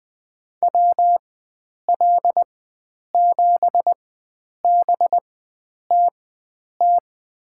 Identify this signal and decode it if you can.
Morse code, heard as a keyed tone, about 20 wpm, spaced as for 9 wpm, 700 Hz. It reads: WL7BTT
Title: USA Call Signs - 4x Character Spacing - 20wpm